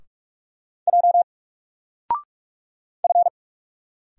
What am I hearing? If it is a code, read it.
Morse code, heard as a keyed tone, about 44 wpm, 700 Hz. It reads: J F
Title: Single Character (Letter:Number:Punctuation:Prosign) 45wpm